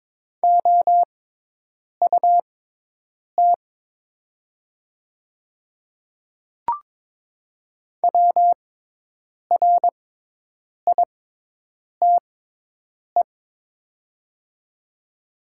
Morse code, heard as a keyed tone, about 22 wpm, spaced as for 8 wpm, 700 Hz. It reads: OUT WRITE
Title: 6x Character Spacing - Top 100 Words - 22wpm